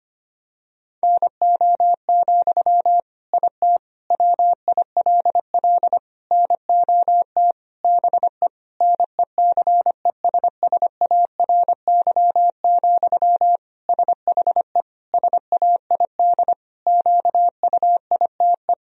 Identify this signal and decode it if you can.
Morse code, heard as a keyed tone, about 25 wpm, 700 Hz. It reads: NO, IT WILL NOT BE NECESSARY, SHE SAID QUITE
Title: Sentences from Top 1000 Words - Review Part 4 - 25wpm